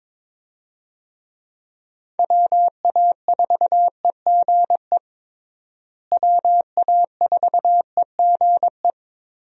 Morse code, heard as a keyed tone, about 22 wpm, 700 Hz. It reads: WA4EGE WA4EGE
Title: USA Call Signs - 6 Characters - Repeated 3x - 22wpm